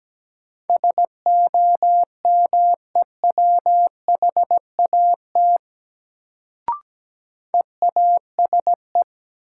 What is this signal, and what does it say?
Morse code, heard as a keyed tone, about 17 wpm, 700 Hz. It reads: SOMEWHAT EASE
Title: Sentences from Top 1000 Words - Review Part 1 - New Word Drill - 17wpm